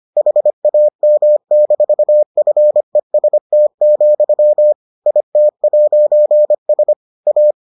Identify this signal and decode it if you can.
Morse code, heard as a keyed tone, about 25 wpm, 600 Hz. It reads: HAM-FEST, IT'S A